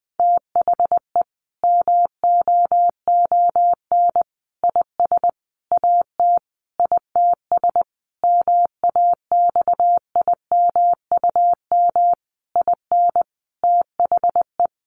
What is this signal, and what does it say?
Morse code, heard as a keyed tone, about 20 wpm, 700 Hz. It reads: THE MOON IS AT ITS MAXIMUM IN THE
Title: Sentences from Top 1300 Words 20wpm